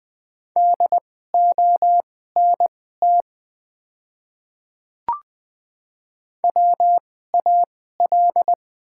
Morse code, heard as a keyed tone, about 20 wpm, spaced as for 14 wpm, 700 Hz. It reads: DONT WAL
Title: 2x Character Spacing - Top 300 Words - 20wpm